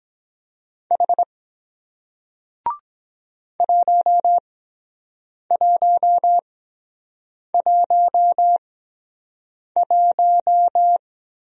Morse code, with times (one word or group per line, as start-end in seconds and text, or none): 0.9-1.2 H
3.6-11.0 1111